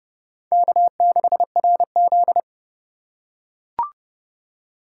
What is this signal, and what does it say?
Morse code, heard as a keyed tone, about 27 wpm, 700 Hz. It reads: K6RZ